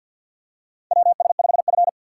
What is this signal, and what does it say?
Morse code, heard as a keyed tone, about 49 wpm, 700 Hz. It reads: WS5F
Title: USA Call Signs - 4 Characters - 6x Character Spacing - 50wpm